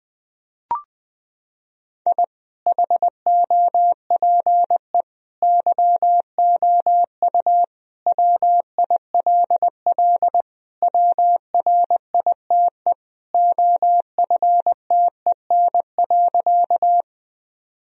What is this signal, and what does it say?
Morse code, heard as a keyed tone, about 21 wpm, 700 Hz. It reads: I HOPE YOU WILL WRITE OFTEN.
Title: Sentences from Top 1000 Words - Review Part 1 - 20wpm